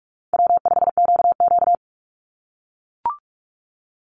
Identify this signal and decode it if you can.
Morse code, heard as a keyed tone, about 44 wpm, 700 Hz. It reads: W5QQ
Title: Call Signs - 45wpm